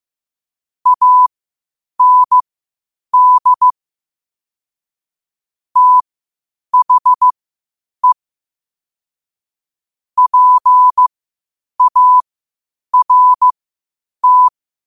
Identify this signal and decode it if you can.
Morse code, heard as a keyed tone, about 15 wpm, spaced as for 8 wpm, 1000 Hz. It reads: AND THE PART
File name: qotdfiles / 8wpm